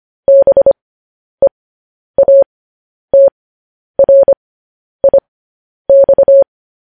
Morse code, heard as a keyed tone, about 25 wpm, spaced as for 10 wpm, 550 Hz. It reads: BEATRIX
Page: news headlines in morse code at 10 wpm